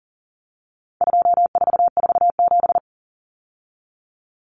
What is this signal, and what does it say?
Morse code, heard as a keyed tone, about 40 wpm, 700 Hz. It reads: 2447
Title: Numbers - 4-Digit 40wpm